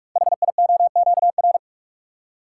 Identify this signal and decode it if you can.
Morse code, heard as a keyed tone, about 44 wpm, 690 Hz. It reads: HIOQR
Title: Code Groups - 5 Letter:Numbers - 45wpm